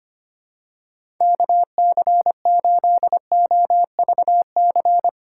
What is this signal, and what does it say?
Morse code, heard as a keyed tone, about 25 wpm, 700 Hz. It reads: KC8OVC
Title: USA Call Signs - 6 Characters - 6x Character Spacing - 25wpm